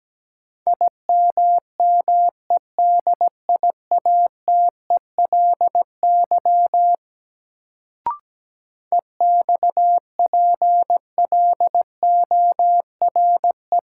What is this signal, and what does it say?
Morse code, heard as a keyed tone, about 17 wpm, 700 Hz. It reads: IMMEDIATELY EXPLORE
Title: Sentences from Top 1100 Words - New Word Drill Alt - 17wpm